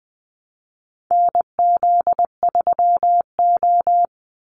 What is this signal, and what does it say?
Morse code, heard as a keyed tone, about 20 wpm, 700 Hz. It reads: NZ3O
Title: Call Signs - 20wpm